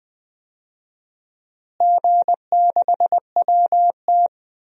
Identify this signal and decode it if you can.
Morse code, heard as a keyed tone, about 21 wpm, 700 Hz. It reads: G6WT
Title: DXCC-100 Call Signs - 4 Characters - Repeated 2x - 20wpm